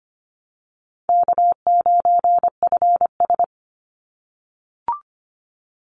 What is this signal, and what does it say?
Morse code, heard as a keyed tone, about 25 wpm, 700 Hz. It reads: K9FS
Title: Easy Call Signs - 25wpm